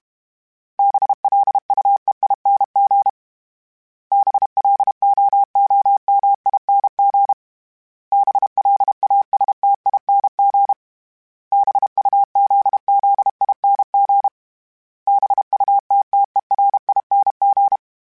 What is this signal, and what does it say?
Morse code, heard as a keyed tone, about 32 wpm, 800 Hz. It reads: BLUEING BLOOMING BLASTING BUZZING BUTTERING
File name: Special INGs-mp3s / Special INGs-mp3s / Sngl Ing @30